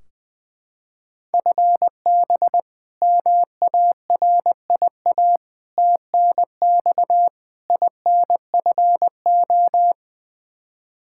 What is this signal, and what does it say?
Morse code, heard as a keyed tone, about 20 wpm, 700 Hz. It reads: FB MARIA TNX INFO
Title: K1USN SST QSO Practice 20wpm